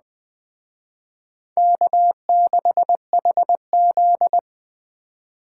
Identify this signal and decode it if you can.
Morse code, heard as a keyed tone, about 19 wpm, 700 Hz. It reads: K6HZ